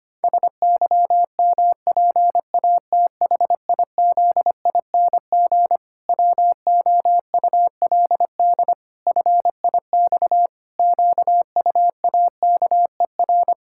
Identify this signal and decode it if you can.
Morse code, heard as a keyed tone, about 25 wpm, 700 Hz. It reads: SYMPATHIZING WOULD FIX QUAKER
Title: Pangram Challenge v2 - 25wpm